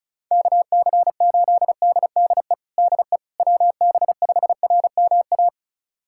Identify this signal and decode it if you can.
Morse code, heard as a keyed tone, about 34 wpm, 690 Hz. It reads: KC8DDE DE WB5RMA
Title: USA Callsign DE Callsign - 6 Characters - 35wpm